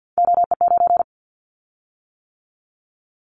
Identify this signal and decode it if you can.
Morse code, heard as a keyed tone, about 49 wpm, 700 Hz. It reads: OE9
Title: Callsign Prefixes for Top 100 DXCC - 50wpm